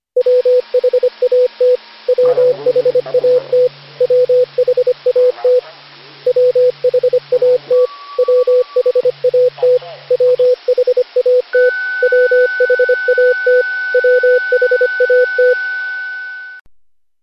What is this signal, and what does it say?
Morse code, heard as a keyed tone, about 25 wpm, 500 Hz, with radio noise. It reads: WHAT WHAT WHAT WHAT WHAT WHAT WHAT WHAT
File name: cw / Session5 cw training